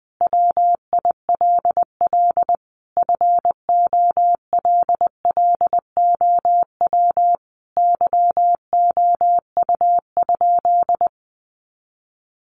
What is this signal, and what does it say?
Morse code, heard as a keyed tone, about 20 wpm, 700 Hz. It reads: WILL FOLLOW YOU?
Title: Sentences from Top 200 Words 20wpm